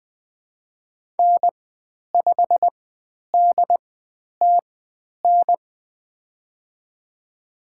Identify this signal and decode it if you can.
Morse code, heard as a keyed tone, about 20 wpm, spaced as for 10 wpm, 700 Hz. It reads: N5DTN
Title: 102 Lesson 17g - Letter u - Call Signs 20wpm